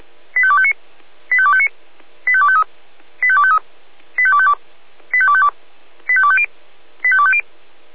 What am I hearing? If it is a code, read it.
Five-tone selective call (ZVEI), heard as quick runs of five tones, tones of about 70 ms. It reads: 85259 85259 85253 85253 85252 85252 85250 85250